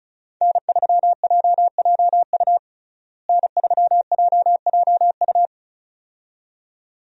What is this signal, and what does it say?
Morse code, heard as a keyed tone, about 36 wpm, 700 Hz. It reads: N3JJU N3JJU
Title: USA Call Signs - 5 Characters - Repeated 2x - 35wpm